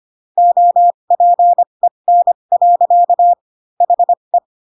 Morse code, heard as a keyed tone, about 25 wpm, spaced as for 22 wpm, 700 Hz. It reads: OPEN. HE
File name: Pinocchio / Pinocchio ch4 18 25 ik1hge